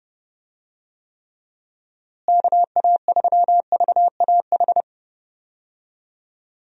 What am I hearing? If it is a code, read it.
Morse code, heard as a keyed tone, about 31 wpm, 700 Hz. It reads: KA3VAH
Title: USA Call Signs - 6 Characters - Repeated 2x - 30wpm